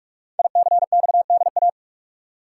Morse code, heard as a keyed tone, about 44 wpm, 700 Hz. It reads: ICXDA
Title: Code Groups - 5 Letter:Numbers - 45wpm